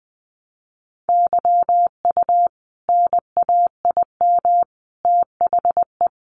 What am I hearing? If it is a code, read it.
Morse code, heard as a keyed tone, about 20 wpm, 700 Hz. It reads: YU NAIM THE
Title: A Sound Of Thunder 20wpm